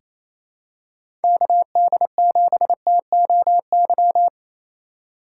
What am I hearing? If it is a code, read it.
Morse code, heard as a keyed tone, about 28 wpm, 700 Hz. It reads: KD7TOY
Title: USA Call Signs - 6 Characters - 3x Character Spacing - 28wpm